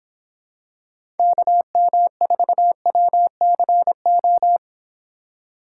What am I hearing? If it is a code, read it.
Morse code, heard as a keyed tone, about 26 wpm, 700 Hz. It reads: KM4WCO